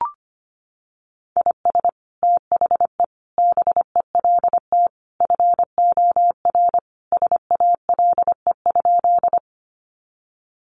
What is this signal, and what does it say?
Morse code, heard as a keyed tone, about 25 wpm, 700 Hz. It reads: IS THE BELT FOR SALE?
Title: Sentences from Top 1400 Words 25wpm